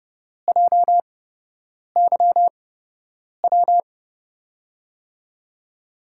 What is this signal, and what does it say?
Morse code, heard as a keyed tone, about 32 wpm, spaced as for 8 wpm, 700 Hz. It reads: JYW
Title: USA Call Signs - 8x Character Spacing - 30wpm